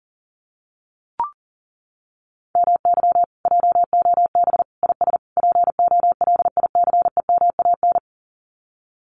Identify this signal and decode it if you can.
Morse code, heard as a keyed tone, about 40 wpm, 700 Hz. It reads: MY JOB IS POLICEMAN
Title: Common QSO Phrases Encore 40wpm